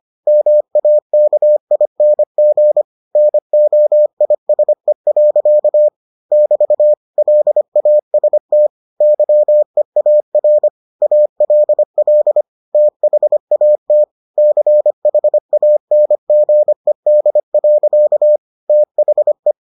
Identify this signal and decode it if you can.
Morse code, heard as a keyed tone, about 25 wpm, 600 Hz. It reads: MAKING NOISE. <BT> LAST YEAR ALL THAT CHANGED. THE